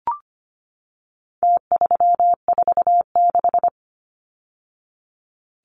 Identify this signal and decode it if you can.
Morse code, heard as a keyed tone, about 25 wpm, 700 Hz. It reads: T346
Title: Numbers - Contest Style - 4-Digit 25wpm